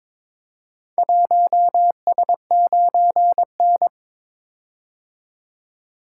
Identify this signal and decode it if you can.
Morse code, heard as a keyed tone, about 22 wpm, 700 Hz. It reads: 1S9N
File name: practice / 22wpm / Four Letter-Number Combinations - 22wpm